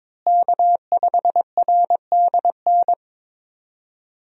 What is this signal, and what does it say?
Morse code, heard as a keyed tone, about 22 wpm, 700 Hz. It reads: K5RDN
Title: USA Call Signs - 5 Characters - Repeated 2x - 22wpm